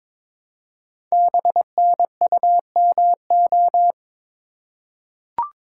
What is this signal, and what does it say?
Morse code, heard as a keyed tone, about 22 wpm, 700 Hz. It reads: BNUMO
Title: Code Groups - 5 Letters - 22wpm